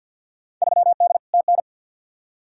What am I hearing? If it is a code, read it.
Morse code, heard as a keyed tone, about 51 wpm, 700 Hz. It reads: 3D TN
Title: Field Day Exchanges Encore - 50wpm